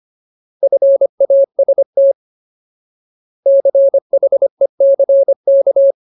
Morse code, heard as a keyed tone, about 25 wpm, 550 Hz. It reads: FAST CHECK
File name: Ultimate Word List / 25WPM Ultimate